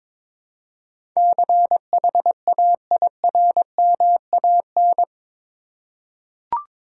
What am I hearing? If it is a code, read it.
Morse code, heard as a keyed tone, about 21 wpm, 700 Hz. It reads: CHAIRMAN
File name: practice / 22wpm / Eight Letter Words - 22wpm